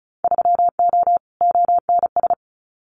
Morse code, heard as a keyed tone, about 34 wpm, 700 Hz. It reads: 3O ONS
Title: Winter Field Day Exchanges Encore 35wpm